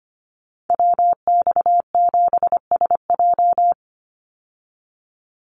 Morse code, heard as a keyed tone, about 25 wpm, 700 Hz. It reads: WX7SJ
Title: Call Signs Repeated 2x - 25wpm